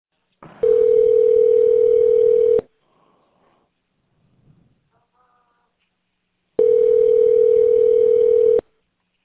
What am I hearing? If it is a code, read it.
Ringback tone (North America), 440 + 480 Hz.